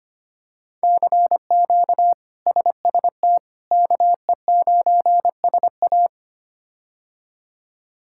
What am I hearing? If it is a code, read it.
Morse code, heard as a keyed tone, about 25 wpm, 700 Hz. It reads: CQ SST KE9SA
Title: K1USN SST QSO Practice - 25wpm